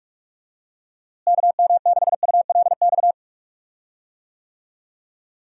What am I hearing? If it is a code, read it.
Morse code, heard as a keyed tone, about 45 wpm, 700 Hz. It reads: KM6ULX
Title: USA Call Signs - 6 Characters - Repeated 2x - 45wpm